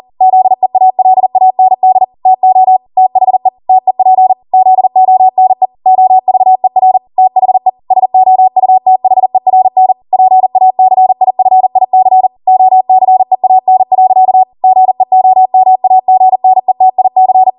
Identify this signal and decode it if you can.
Morse code, heard as a keyed tone, about 40 wpm, 750 Hz. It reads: ZEALAND TO THE TEP ZONE OVER THE SOUTHERN PACIFIC OCEAN. GEOMAGNETIC